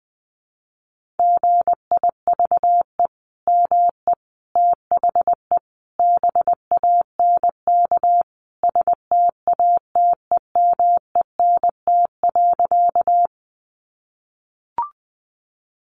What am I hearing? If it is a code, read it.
Morse code, heard as a keyed tone, about 20 wpm, 700 Hz. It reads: GIVE ME THE BANK STATEMENT.
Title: Sentences from Top 1200 Words 20wpm